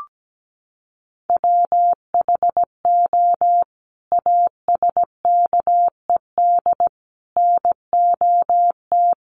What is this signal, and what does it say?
Morse code, heard as a keyed tone, about 17 wpm, 700 Hz. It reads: WHO ASKED NOT
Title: Sets of 5 Words Encore2 17wpm